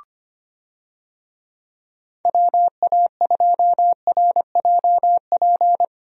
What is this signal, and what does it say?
Morse code, heard as a keyed tone, about 25 wpm, 700 Hz. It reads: WA2RJP
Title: USA Call Signs - 6 Characters - Repeated 2x - 25wpm